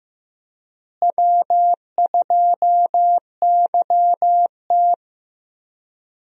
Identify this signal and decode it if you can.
Morse code, heard as a keyed tone, about 15 wpm, 700 Hz. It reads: W2YT
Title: USA Call Signs - 4 Characters - Repeated 3x - 15wpm